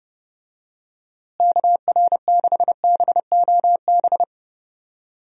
Morse code, heard as a keyed tone, about 31 wpm, 700 Hz. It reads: KR6BOB
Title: USA Call Signs - 6 Characters - Repeated 3x - 30wpm